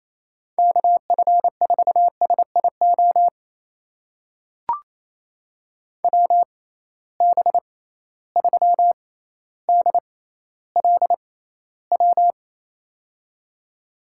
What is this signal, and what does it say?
Morse code, heard as a keyed tone, about 28 wpm, spaced as for 14 wpm, 700 Hz. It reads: KF4SIO WB3DLW
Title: USA Call Signs - 6 Characters - 6x Character Spacing - 28wpm